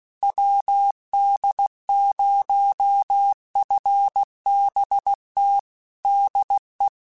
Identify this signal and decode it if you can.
Morse code, heard as a keyed tone, about 16 wpm, 800 Hz. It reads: WD0FBT DE
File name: QSOs / QSOs-mp3s / WD0FBT 1@15